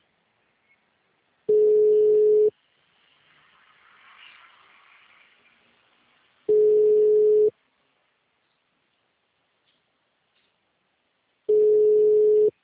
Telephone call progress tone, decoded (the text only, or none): ringback tone